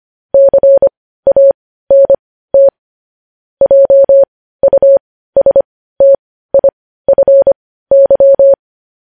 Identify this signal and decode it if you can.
Morse code, heard as a keyed tone, about 25 wpm, spaced as for 15 wpm, 550 Hz. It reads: CANT JUSTIFY
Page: news-headlines-in-morse-code-at-15-wpm